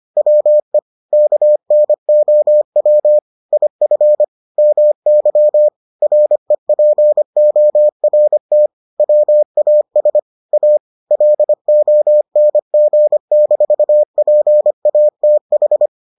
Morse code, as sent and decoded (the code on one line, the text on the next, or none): .-- . / -.- -. --- .-- / .. ..-. / -- -.-- / .-. .--. --- .-. - / .-- .- ... / .- / .-.. --- -. --. -....- .--. .- - ....
WE KNOW IF MY REPORT WAS A LONG-PATH